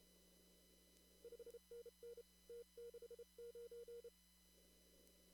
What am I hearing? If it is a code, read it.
Morse code, heard as a keyed tone, about 28 wpm, 470 Hz. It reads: 5NN T69
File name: CW / kt5j wpx cw-op-k5pi